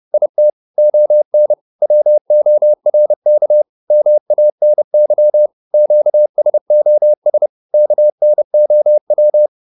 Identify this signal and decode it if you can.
Morse code, heard as a keyed tone, about 30 wpm, 600 Hz. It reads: IT ON WORK MANY QSOS KNOW